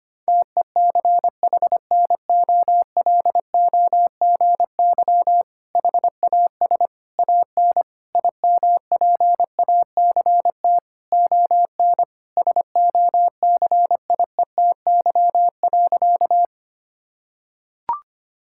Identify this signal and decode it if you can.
Morse code, heard as a keyed tone, about 25 wpm, 700 Hz. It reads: TECHNOLOGY HAS AN IMPACT ON SOCIETY.
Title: Sentences from Top 1000 Words - Review Part 6 - 25wpm